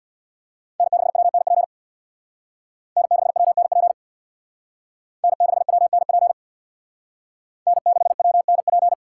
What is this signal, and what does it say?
Morse code, heard as a keyed tone, about 58 wpm, 700 Hz. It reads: N6WNP N6WNP N6WNP N6WNP
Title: Speed Racing - USA 1x3 Call Signs 50wpm